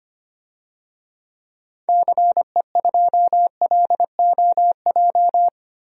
Morse code, heard as a keyed tone, about 25 wpm, 700 Hz. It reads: CE2LOJ
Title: DXCC-100 Call Signs - 6 Characters - Repeated 3x - 25wpm